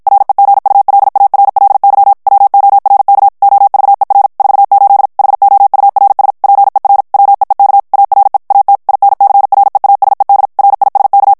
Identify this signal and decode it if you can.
Morse code, heard as a keyed tone, about 55 wpm, 800 Hz. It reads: REGULARLY WORK OVER 48 HOURS PER WEEK ARE AT INCREASED RISK
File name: morse mp3 / ANGLAIS texte clair 1 vitesse 50 mots minute